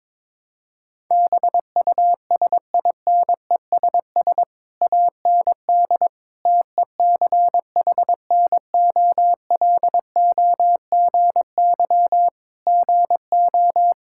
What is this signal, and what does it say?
Morse code, heard as a keyed tone, about 22 wpm, 700 Hz. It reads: BUSINESS AND TECHNOLOGY GO
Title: Sentences from Top 1000 Words - Review Part 5 - 22wpm